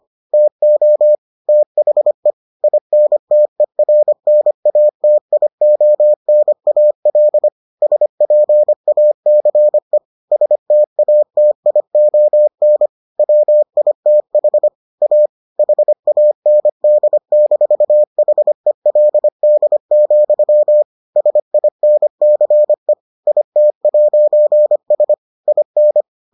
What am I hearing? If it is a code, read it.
Morse code, heard as a keyed tone, about 25 wpm, 600 Hz. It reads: TO THE INTERNATIONAL SPACE STATION WITH A HAND-HELD, SINCE IT'S IN